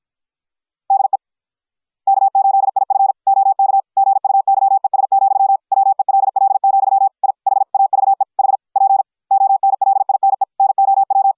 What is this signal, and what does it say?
Morse code, heard as a keyed tone, about 53 wpm, 750 Hz. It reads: DE K8IF OK GUYES, WELL, I HAVE HP ONLINE NOW